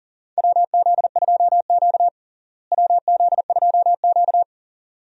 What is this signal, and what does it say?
Morse code, heard as a keyed tone, about 40 wpm, 700 Hz. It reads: WZ2Q WZ2Q
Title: USA Call Signs - 4 Characters - Repeated 2x - 40wpm